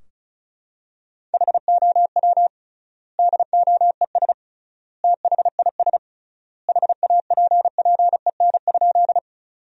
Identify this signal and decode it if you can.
Morse code, heard as a keyed tone, about 36 wpm, 700 Hz. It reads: HOW DOES THIS HAPPEN?
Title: 3x Word-Spacing - Sentences from Top 400 Words - 35wpm